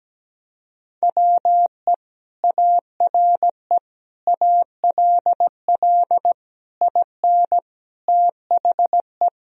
Morse code, heard as a keyed tone, about 17 wpm, 700 Hz. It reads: WE ARE ALL IN THE
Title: Sentences from Top 1000 Words - Review Part 3 - 17wpm